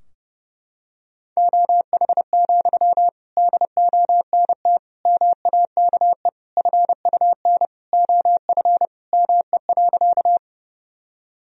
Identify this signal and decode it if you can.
Morse code, heard as a keyed tone, about 30 wpm, 700 Hz. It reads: OH, DONT MAKE FUN OF ME.
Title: Sentences from Top 1000 Words - Review Part 4 - 30wpm